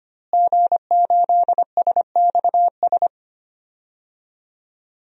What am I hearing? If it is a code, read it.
Morse code, heard as a keyed tone, about 25 wpm, 700 Hz. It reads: G8SXS
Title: DXCC-100 Call Signs - 5 Characters - Repeated 3x - 25wpm